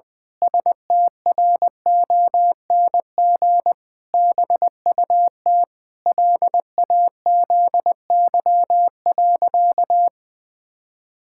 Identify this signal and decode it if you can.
Morse code, heard as a keyed tone, about 21 wpm, 700 Hz. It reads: STRONG BUT LAZY.
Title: Pangram Challenge v2 - 20wpm